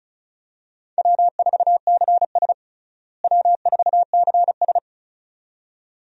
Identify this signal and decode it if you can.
Morse code, heard as a keyed tone, about 36 wpm, 700 Hz. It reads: W4CS W4CS
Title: USA Call Signs - 4 Characters - Repeated 2x - 35wpm